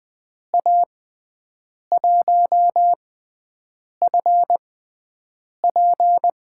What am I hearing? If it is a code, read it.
Morse code, heard as a keyed tone, about 20 wpm, spaced as for 7 wpm, 700 Hz. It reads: A1FP
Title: USA Call Signs - 5 Characters - 6x Character Spacing - 20wpm